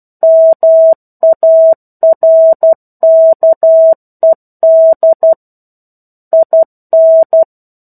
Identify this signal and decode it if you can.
Morse code, heard as a keyed tone, about 12 wpm, 650 Hz. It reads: MARKED IN
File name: 01-Chapter-01 12 WPM